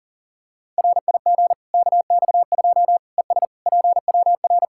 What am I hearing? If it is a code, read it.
Morse code, heard as a keyed tone, about 40 wpm, 700 Hz. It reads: RIG KX2 ES PWR
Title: Practice QSO - Exchange 2 - Info - 40wpm